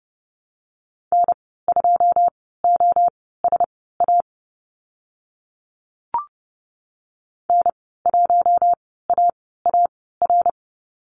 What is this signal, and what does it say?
Morse code, heard as a keyed tone, about 30 wpm, spaced as for 17 wpm, 700 Hz. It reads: N2OSA N1AAR
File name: Lesson 15g - Number 5 - Call Signs 30wpm